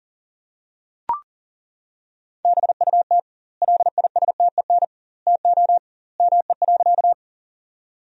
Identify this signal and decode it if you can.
Morse code, heard as a keyed tone, about 40 wpm, 700 Hz. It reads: BUT LISTEN TO ME.
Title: 2x Word-Spacing - Sentences from Top 400 Words - 40wpm